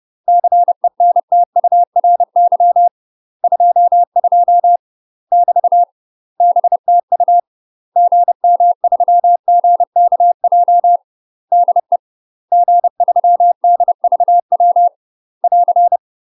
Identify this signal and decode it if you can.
Morse code, heard as a keyed tone, about 30 wpm, 700 Hz. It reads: CENTURY 22 <BT> BTU GM3GKJ DE G3DVW <AR>
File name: UK MASTER QSOs TRAINING / 30WPM / UK MASTER QSOs 0021